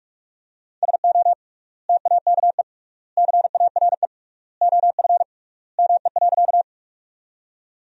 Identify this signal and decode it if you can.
Morse code, heard as a keyed tone, about 42 wpm, 700 Hz. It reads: SO TAKE CARE OF ME.
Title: 3x Word-Spacing - Sentences from Top 300 Words - 45wpm